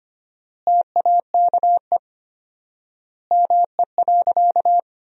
Morse code, heard as a keyed tone, about 25 wpm, 700 Hz. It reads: TAKE ME.
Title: 4x Word-Spacing - Sentences from Top 400 Words - 25wpm